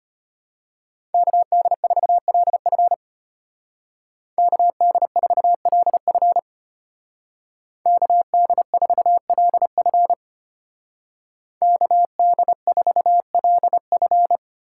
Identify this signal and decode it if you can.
Morse code, heard as a keyed tone, about 30 wpm, 700 Hz. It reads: KD4LF KD4LF KD4LF KD4LF
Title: Speed Racing - USA 2x2 Call Signs 25wpm